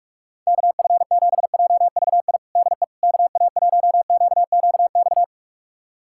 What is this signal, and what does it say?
Morse code, heard as a keyed tone, about 46 wpm, 700 Hz. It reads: KF7JVI DE KA1QQX
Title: USA Callsign DE Callsign - 6 Characters - 45wpm